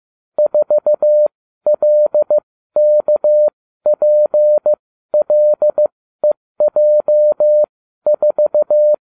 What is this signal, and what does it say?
Morse code, heard as a keyed tone, about 15 wpm, spaced as for 12 wpm, 600 Hz. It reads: 4LKPLEJ4